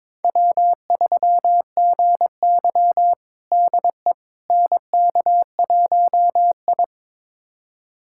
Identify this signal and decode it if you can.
Morse code, heard as a keyed tone, about 21 wpm, 700 Hz. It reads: W3GY DE NK1I
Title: USA Callsign DE Callsign - 4 Characters - 22wpm